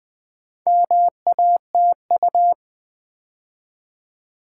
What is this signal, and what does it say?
Morse code, heard as a keyed tone, about 21 wpm, 700 Hz. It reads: MATU
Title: Build A Word - 2 Letters At A Time - 6-Letter Words - 20wpm